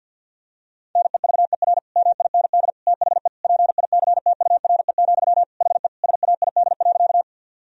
Morse code, heard as a keyed tone, about 50 wpm, 700 Hz. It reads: NEVER MIND THE PICTURE, HE SAID.
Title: Sentences from Top 1000 Words - Review Part 1 - 50wpm